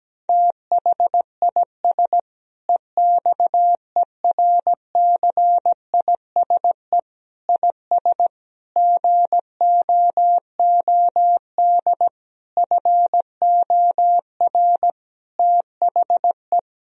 Morse code, heard as a keyed tone, about 17 wpm, 700 Hz. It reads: THIS EXERCISE IS GOOD FOR THE
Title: Sentences from Top 1000 Words - Review Part 4 - 17wpm